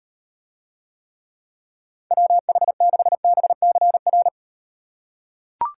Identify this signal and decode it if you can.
Morse code, heard as a keyed tone, about 37 wpm, 700 Hz. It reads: WH6BCR